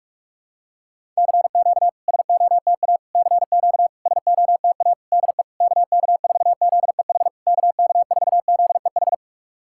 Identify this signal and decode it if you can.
Morse code, heard as a keyed tone, about 42 wpm, 690 Hz. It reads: CQ SOTA CQ SOTA DE KK4ZEH KK4ZEH
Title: SOTA QSO Practice 45wpm